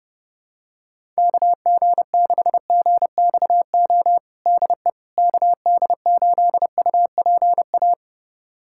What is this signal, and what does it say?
Morse code, heard as a keyed tone, about 30 wpm, 700 Hz. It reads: KG6GXO DE KD8UPA
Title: USA Callsign DE Callsign - 6 Characters - 30wpm